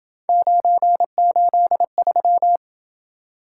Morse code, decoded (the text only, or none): 983